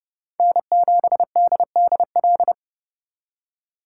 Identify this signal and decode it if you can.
Morse code, heard as a keyed tone, about 38 wpm, 700 Hz. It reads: N7DDL